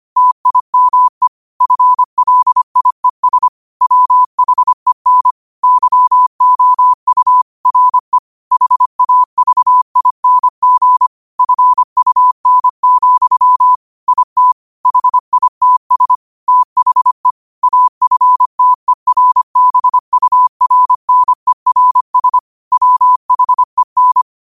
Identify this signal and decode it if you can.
Morse code, heard as a keyed tone, about 25 wpm, 1000 Hz. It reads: TIME FLIES WHEN YOU RE HAVING FUN, IT HITS THE AFTERBURNERS WHEN
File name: qotdfiles / 25wpm